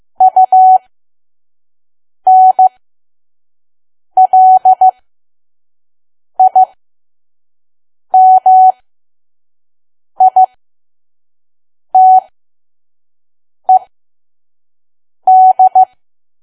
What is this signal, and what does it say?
Morse code, heard as a keyed tone, about 15 wpm, spaced as for 5 wpm, 750 Hz. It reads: UNLIMITED
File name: Archive / 5 WPM files / PM